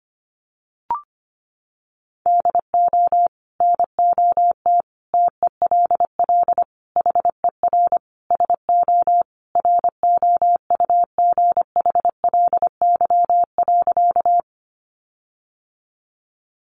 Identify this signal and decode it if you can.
Morse code, heard as a keyed tone, about 25 wpm, 700 Hz. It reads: DO NOT TELL HER SO ROUGHLY.
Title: Sentences from Top 1500 Words 25wpm